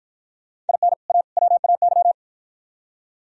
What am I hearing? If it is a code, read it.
Morse code, heard as a keyed tone, about 51 wpm, 690 Hz. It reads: IN A WAY